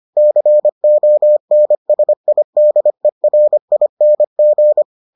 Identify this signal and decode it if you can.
Morse code, heard as a keyed tone, about 25 wpm, 600 Hz. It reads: CONSIDERING